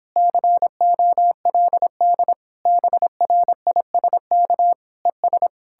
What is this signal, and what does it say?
Morse code, heard as a keyed tone, about 31 wpm, 700 Hz. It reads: COLD BRISK ES